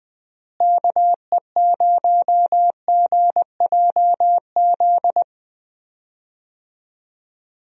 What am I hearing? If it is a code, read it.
Morse code, heard as a keyed tone, about 21 wpm, 700 Hz. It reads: KE0GJZ
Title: USA Call Signs - 6 Characters - Repeated 2x - 20wpm